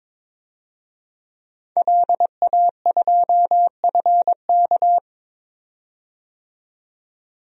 Morse code, heard as a keyed tone, about 21 wpm, 700 Hz. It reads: LA2FK
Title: DXCC-100 Call Signs - 5 Characters - 22wpm